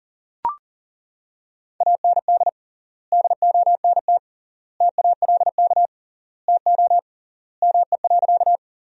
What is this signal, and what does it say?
Morse code, heard as a keyed tone, about 40 wpm, 700 Hz. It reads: AND DONT TALK TO ME.
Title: 3x Word-Spacing - Sentences from Top 400 Words - 40wpm